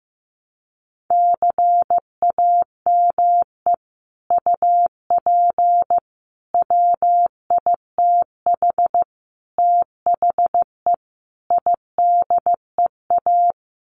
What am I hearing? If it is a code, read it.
Morse code, heard as a keyed tone, about 15 wpm, 700 Hz. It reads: CAME UP WITH THE IDEA
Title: Sets of 5 Words Encore2 15wpm